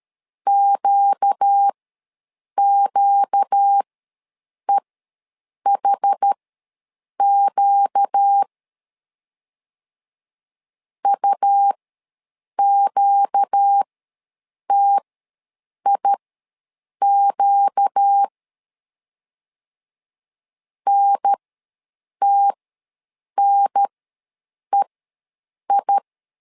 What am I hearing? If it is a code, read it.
Morse code, heard as a keyed tone, about 13 wpm, spaced as for 7 wpm, 800 Hz. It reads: QQEHQ UQTIQ NTNEI
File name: LEZIONI / 4 Q ws 12wpm 1912